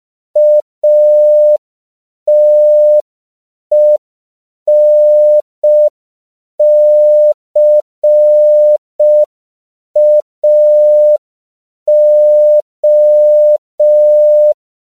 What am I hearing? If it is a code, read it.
Morse code, heard as a keyed tone, about 5 wpm, 600 Hz. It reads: ATENCAO